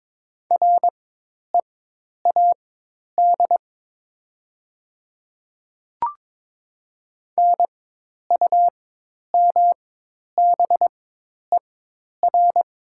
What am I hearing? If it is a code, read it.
Morse code, heard as a keyed tone, about 22 wpm, spaced as for 10 wpm, 700 Hz. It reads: READ NUMBER
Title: 4x Character Spacing - Top 300 Words - 22wpm